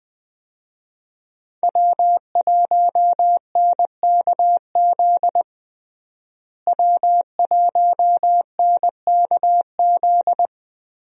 Morse code, heard as a keyed tone, about 20 wpm, 700 Hz. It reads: W1NKZ W1NKZ
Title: USA Call Signs - 5 Characters - Repeated 3x - 20wpm